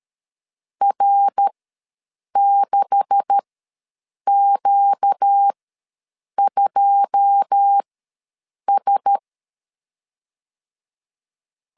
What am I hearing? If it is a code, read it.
Morse code, heard as a keyed tone, about 13 wpm, spaced as for 7 wpm, 800 Hz. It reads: R6Q2S
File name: LEZIONI / 9 286 ws 12wpm 3351